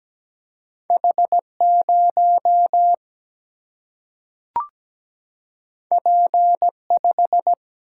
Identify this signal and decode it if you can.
Morse code, heard as a keyed tone, about 17 wpm, 700 Hz. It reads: H0 P5
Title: Single-Letter Single-Number 17wpm